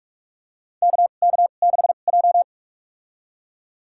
Morse code, heard as a keyed tone, about 44 wpm, spaced as for 32 wpm, 700 Hz. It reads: KK6J
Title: USA Call Signs - 4 Characters - 2x Character Spacing - 45wpm